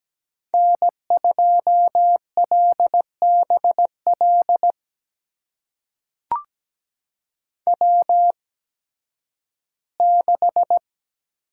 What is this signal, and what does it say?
Morse code, heard as a keyed tone, about 17 wpm, spaced as for 11 wpm, 700 Hz. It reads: N2LBL W6
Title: USA Call Signs - 8x Character Spacing - 17wpm